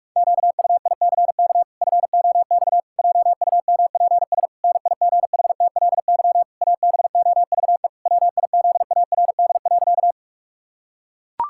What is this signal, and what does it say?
Morse code, heard as a keyed tone, about 41 wpm, 700 Hz. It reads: QUICK FOX JUMPS NIGHTLY ABOVE WIZARD.
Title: Pangram Challenge v2 - 45wpm